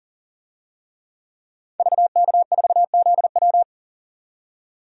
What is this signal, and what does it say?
Morse code, heard as a keyed tone, about 40 wpm, 700 Hz. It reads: VK4ZW
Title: DXCC-100 Call Signs - 5 Characters - 40wpm